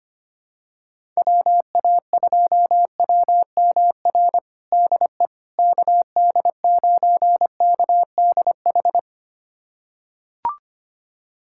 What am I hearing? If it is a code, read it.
Morse code, heard as a keyed tone, about 25 wpm, 700 Hz. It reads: WA2WMR DE KD9KDH
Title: USA Callsign DE Callsign - 6 Characters - 25wpm